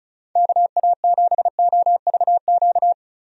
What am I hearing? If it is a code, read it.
Morse code, heard as a keyed tone, about 36 wpm, 700 Hz. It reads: KA7OVQ
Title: USA Call Signs - 6 Characters - 35wpm